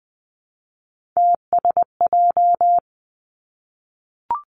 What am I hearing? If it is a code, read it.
Morse code, heard as a keyed tone, about 20 wpm, 700 Hz. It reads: TSJ
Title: Three Challenging Letter-Number Combinations 20wpm